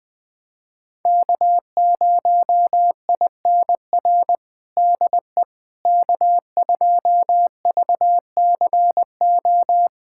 Morse code, heard as a keyed tone, about 20 wpm, 700 Hz. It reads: K0INR DE K2VCO
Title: USA Callsign DE Callsign - 5 Characters - 20wpm